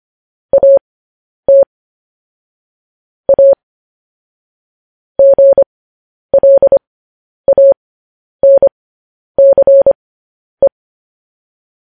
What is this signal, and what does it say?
Morse code, heard as a keyed tone, about 25 wpm, spaced as for 10 wpm, 550 Hz. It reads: AT A GLANCE
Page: news-headlines-in-morse-code-at-10-wpm